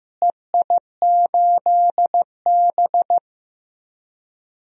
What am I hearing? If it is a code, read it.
Morse code, heard as a keyed tone, about 16 wpm, 700 Hz. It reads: EI8B
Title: DXCC-100 Call Signs - 4 Characters - Repeated 3x - 15wpm